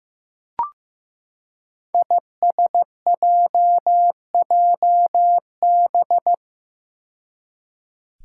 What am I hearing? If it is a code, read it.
Morse code, heard as a keyed tone, about 15 wpm, 700 Hz. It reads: ISJJB
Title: Five Challenging Letter-Number Combinations 15wpm